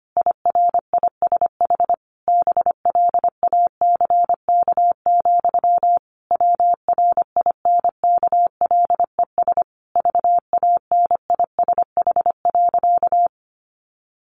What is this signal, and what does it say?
Morse code, heard as a keyed tone, about 25 wpm, 700 Hz. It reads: IRISH BLACK, WRINKLES VANISH.
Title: A Sound Of Thunder 25wpm